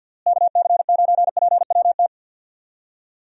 Morse code, heard as a keyed tone, about 52 wpm, 700 Hz. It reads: KC9PWT
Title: USA Call Signs - 6 Characters - Repeated 2x - 50wpm